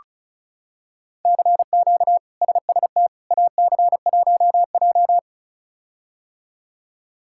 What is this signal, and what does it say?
Morse code, heard as a keyed tone, about 34 wpm, 700 Hz. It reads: CQ SST AC1J
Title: K1USN SST QSO Practice - 35wpm